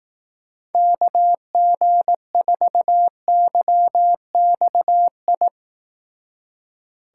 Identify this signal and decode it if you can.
Morse code, heard as a keyed tone, about 18 wpm, 700 Hz. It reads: KG4YXI
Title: Speed Racing - USA 2x3 Call Signs 15wpm